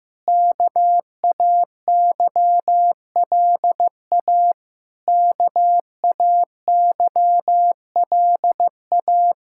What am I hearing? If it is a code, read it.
Morse code, heard as a keyed tone, about 15 wpm, 700 Hz. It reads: KAYLA KAYLA
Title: Practice QSO - Exchange 1 - Beginning - 15wpm